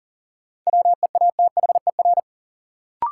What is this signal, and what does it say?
Morse code, heard as a keyed tone, about 40 wpm, 700 Hz. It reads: WEATHER
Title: Sentences from Top 800 Words - New Word Drill - 40wpm